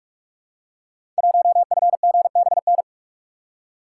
Morse code, heard as a keyed tone, about 44 wpm, 700 Hz. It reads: 1FGBN
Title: Code Groups - 5 Letter:Numbers - 45wpm